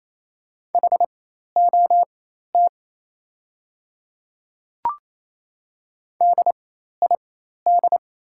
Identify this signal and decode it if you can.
Morse code, heard as a keyed tone, about 28 wpm, spaced as for 13 wpm, 700 Hz. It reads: HOT DID
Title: 4x Character Spacing - Top 200 Words - 28wpm